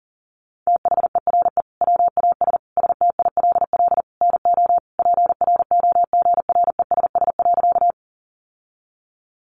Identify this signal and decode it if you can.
Morse code, heard as a keyed tone, about 40 wpm, 700 Hz. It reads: THERE WAS STILL NO PROGRESS.
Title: Sentences from Top 1400 Words 40wpm